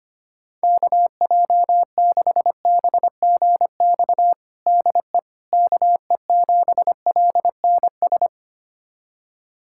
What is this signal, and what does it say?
Morse code, heard as a keyed tone, about 25 wpm, 700 Hz. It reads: KJ6BGX DE KE7LNS
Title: USA Callsign DE Callsign - 6 Characters - 25wpm